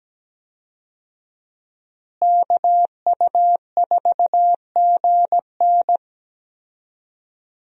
Morse code, heard as a keyed tone, about 17 wpm, 700 Hz. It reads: KU4GN
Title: USA Call Signs - 5 Characters - Repeated 3x - 17wpm